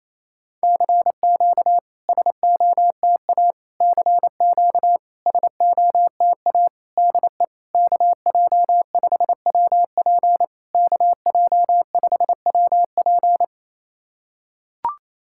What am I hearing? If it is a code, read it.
Morse code, heard as a keyed tone, about 28 wpm, 690 Hz. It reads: CQ SOTA CQ SOTA DE KJ5WP KJ5WP
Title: SOTA QSO Practice 28wpm